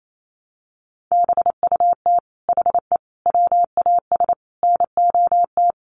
Morse code, heard as a keyed tone, about 28 wpm, 700 Hz. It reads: BUT HE WAS NOT
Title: Sentences from Top 1500 Words 28wpm